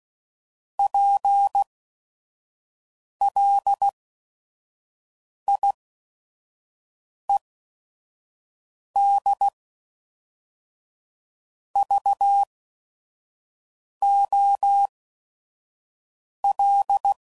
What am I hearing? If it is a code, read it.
Morse code, heard as a keyed tone, about 16 wpm, 800 Hz. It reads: P L I E D V O L